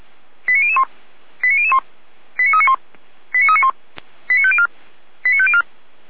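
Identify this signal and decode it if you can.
Five-tone selective call (ZVEI), heard as quick runs of five tones, tones of about 70 ms. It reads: 89001 89001 89391 89391 89504 89504
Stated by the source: ZVEI five-tone call received by SDR.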